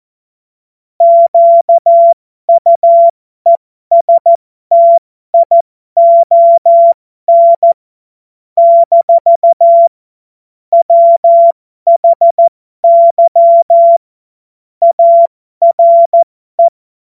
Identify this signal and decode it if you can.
Morse code, heard as a keyed tone, about 14 wpm, spaced as for 12 wpm, 680 Hz. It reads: QUESTION - WHY ARE
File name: Morsekurs / Witze